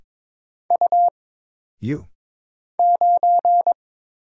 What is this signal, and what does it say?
Morse code, heard as a keyed tone, about 23 wpm, 700 Hz. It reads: U 9
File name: Single Character (Letter:Number:Punctuation:Prosign) - Mind-Melt - 22wpm